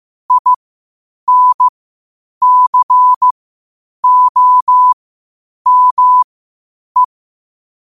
Morse code, heard as a keyed tone, about 15 wpm, spaced as for 8 wpm, 1000 Hz. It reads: INCOME
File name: qotdfiles / 8wpm